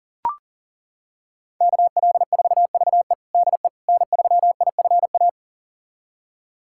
Morse code, heard as a keyed tone, about 40 wpm, 700 Hz. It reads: KL4VE DE N3IFA
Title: USA Callsign DE Callsign - 5 Characters - 40wpm